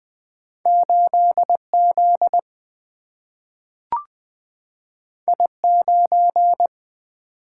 Morse code, heard as a keyed tone, about 20 wpm, 700 Hz. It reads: 8Z I9
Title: Single-Letter Single-Number 20wpm